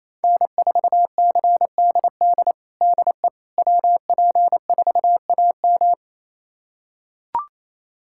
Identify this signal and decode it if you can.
Morse code, heard as a keyed tone, about 27 wpm, 700 Hz. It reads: N4CDD DE WP4AM
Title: USA Callsign DE Callsign - 5 Characters - 28wpm